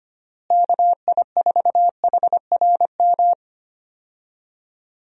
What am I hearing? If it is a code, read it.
Morse code, heard as a keyed tone, about 25 wpm, 700 Hz. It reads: KI4HRM
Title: USA Call Signs - 6 Characters - Repeated 3x - 25wpm